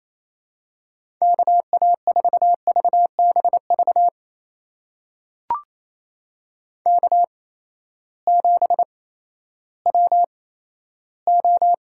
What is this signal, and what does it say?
Morse code, heard as a keyed tone, about 28 wpm, spaced as for 14 wpm, 700 Hz. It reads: KA4VBV K7WO
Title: USA Call Signs - 8x Character Spacing - 28wpm